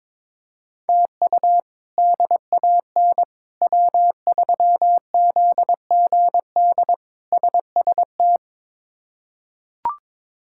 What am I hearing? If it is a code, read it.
Morse code, heard as a keyed tone, about 22 wpm, 700 Hz. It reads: TU DAN W3ZGD SST
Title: K1USN SST QSO Practice - 22wpm